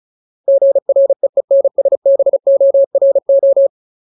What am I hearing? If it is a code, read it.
Morse code, heard as a keyed tone, about 34 wpm, 550 Hz. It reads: GREENSBORO
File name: Ultimate Word List / 35WPM Ultimate